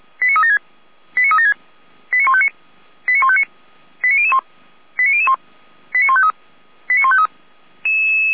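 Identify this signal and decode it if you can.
Five-tone selective call (ZVEI), heard as quick runs of five tones, tones of about 70 ms. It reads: 89376 89376 89159 89159 89001 89001 89153 89153
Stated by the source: ZVEI five-tone selective call received on an SDR, recorded by the feuerwehr-orlamuende fire brigade.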